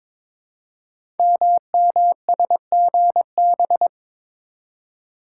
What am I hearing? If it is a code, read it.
Morse code, heard as a keyed tone, about 22 wpm, 700 Hz. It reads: MMSGB
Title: Code Groups - 5 Letters - 22wpm